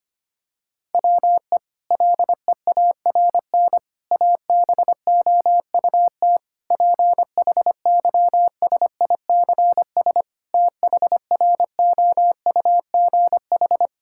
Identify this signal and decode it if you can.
Morse code, heard as a keyed tone, about 25 wpm, 700 Hz. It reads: WE LEARN ABOUT PHYSICS THROUGH
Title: Sentences from Top 1000 Words - Review Part 5 - 25wpm